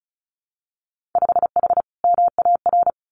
Morse code, heard as a keyed tone, about 36 wpm, 700 Hz. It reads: 5H MAR
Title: Winter Field Day Exchanges Encore 35wpm